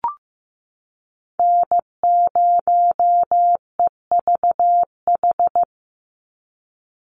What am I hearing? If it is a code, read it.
Morse code, heard as a keyed tone, about 15 wpm, 700 Hz. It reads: N0EVH
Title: NA SOTA Chaser Callsigns 15wpm